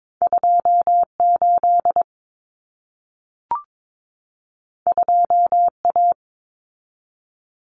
Morse code, heard as a keyed tone, about 22 wpm, 700 Hz. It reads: 28 2A
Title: Common Cut Numbers - 2-Digit 22wpm